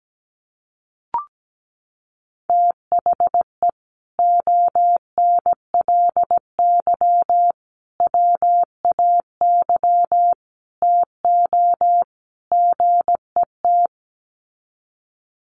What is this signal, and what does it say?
Morse code, heard as a keyed tone, about 17 wpm, 700 Hz. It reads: THE ONLY WAY TO GET
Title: Sets of 5 Words Encore2 17wpm